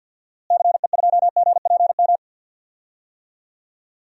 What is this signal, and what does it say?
Morse code, heard as a keyed tone, about 52 wpm, 700 Hz. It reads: CE2GPM
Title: DXCC-100 Call Signs - 6 Characters - Repeated 3x - 50wpm